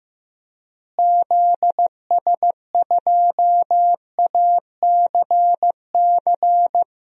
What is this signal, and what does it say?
Morse code, heard as a keyed tone, about 16 wpm, 700 Hz. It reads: ZS2ACC
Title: DXCC-100 Call Signs - 6 Characters - Repeated 3x - 15wpm